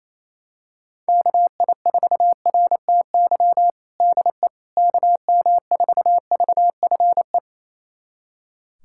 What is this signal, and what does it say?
Morse code, heard as a keyed tone, about 27 wpm, 700 Hz. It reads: KI4RTY DE KM4VFE
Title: USA Callsign DE Callsign - 6 Characters - 28wpm